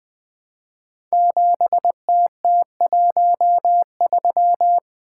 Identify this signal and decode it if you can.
Morse code, heard as a keyed tone, about 20 wpm, 700 Hz. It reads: 7TT13
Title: Code Groups - 5 Letter:Numbers - 20wpm